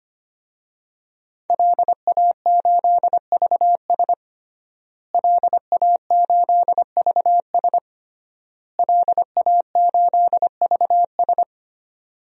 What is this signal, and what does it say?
Morse code, heard as a keyed tone, about 25 wpm, 700 Hz. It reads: LA8VS LA8VS LA8VS
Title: DXCC-100 Call Signs - 5 Characters - Repeated 3x - 25wpm